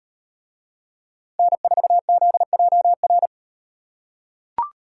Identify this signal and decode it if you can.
Morse code, heard as a keyed tone, about 38 wpm, 700 Hz. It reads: N4ZJR